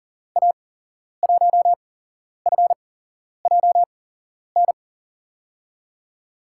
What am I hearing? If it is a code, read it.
Morse code, heard as a keyed tone, about 40 wpm, spaced as for 11 wpm, 700 Hz. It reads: A1FJN
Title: USA Call Signs - 6 Characters - 8x Character Spacing - 40wpm